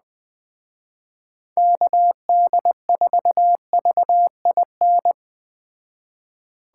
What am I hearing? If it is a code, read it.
Morse code, heard as a keyed tone, about 20 wpm, 700 Hz. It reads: KD4VIN
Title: Speed Racing - USA 2x3 Call Signs 17wpm